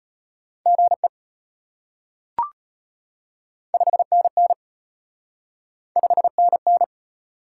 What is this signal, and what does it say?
Morse code, heard as a keyed tone, about 33 wpm, 700 Hz. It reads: GE 5NN 5NN